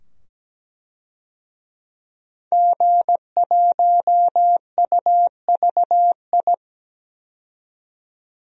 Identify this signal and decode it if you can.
Morse code, heard as a keyed tone, about 17 wpm, 700 Hz. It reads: G1UVI
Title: DXCC-100 Call Signs - 5 Characters - 17wpm